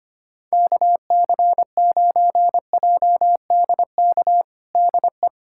Morse code, heard as a keyed tone, about 25 wpm, 700 Hz. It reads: KC9JDK DE
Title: USA Callsign DE Callsign - 6 Characters - 25wpm